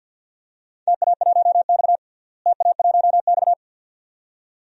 Morse code, heard as a keyed tone, about 53 wpm, 700 Hz. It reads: TA1X TA1X
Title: DXCC-100 Call Signs - 4 Characters - Repeated 2x - 50wpm